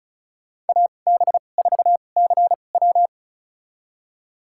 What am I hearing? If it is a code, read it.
Morse code, heard as a keyed tone, about 36 wpm, spaced as for 25 wpm, 700 Hz. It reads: AB4CW
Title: USA Call Signs - 5 Characters - 2x Character Spacing - 35wpm